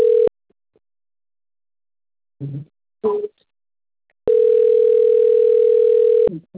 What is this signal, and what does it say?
Ringback tone (North America), 440 + 480 Hz.
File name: out- - - - - .68056